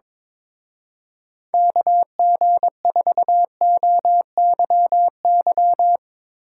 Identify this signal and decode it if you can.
Morse code, heard as a keyed tone, about 23 wpm, 700 Hz. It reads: KG4OYY
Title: Speed Racing - USA 2x3 Call Signs 22wpm